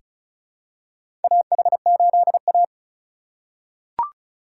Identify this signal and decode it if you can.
Morse code, heard as a keyed tone, about 36 wpm, 700 Hz. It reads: AH8A
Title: USA Call Signs - 2x Character Spacing - 35wpm